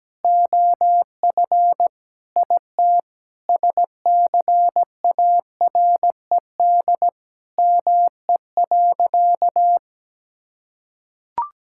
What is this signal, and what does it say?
Morse code, heard as a keyed tone, about 17 wpm, 700 Hz. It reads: OF IT SCARED ME.
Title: Sentences from Top 1000 Words - Review Part 5 - 17wpm